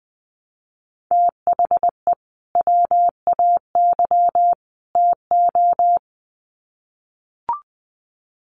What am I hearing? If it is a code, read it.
Morse code, heard as a keyed tone, about 20 wpm, 700 Hz. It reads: THE WAY TO
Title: Sets of 3 Words 20wpm